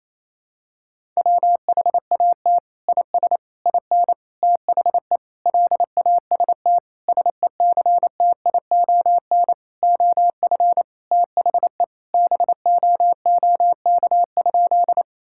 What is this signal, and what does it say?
Morse code, heard as a keyed tone, about 28 wpm, 700 Hz. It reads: WHAT IS IN THE LAST SECTION OF THE BOOK?
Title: Sentences from Top 1000 Words - Review Part 5 - 28wpm